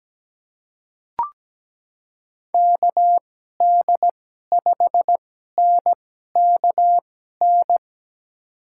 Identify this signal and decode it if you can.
Morse code, heard as a keyed tone, about 17 wpm, spaced as for 12 wpm, 700 Hz. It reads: KD5NKN
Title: USA Call Signs - 6 Characters - 2x Character Spacing - 17wpm